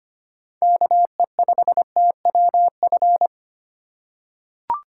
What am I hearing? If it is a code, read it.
Morse code, heard as a keyed tone, about 25 wpm, 700 Hz. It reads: KE5TWF
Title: USA Call Signs - 6 Characters - 3x Character Spacing - 25wpm